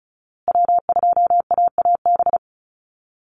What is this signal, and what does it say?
Morse code, heard as a keyed tone, about 34 wpm, 700 Hz. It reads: W2AAB
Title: Call Signs Repeated 2x - 35wpm